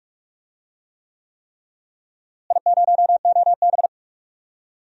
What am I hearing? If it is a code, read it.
Morse code, heard as a keyed tone, about 47 wpm, 700 Hz. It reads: I0OB
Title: DXCC-100 Call Signs - 4 Characters - Repeated 3x - 45wpm